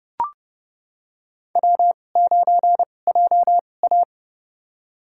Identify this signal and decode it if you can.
Morse code, heard as a keyed tone, about 30 wpm, spaced as for 22 wpm, 700 Hz. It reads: W9JA
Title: USA Call Signs - 2x Character Spacing - 30wpm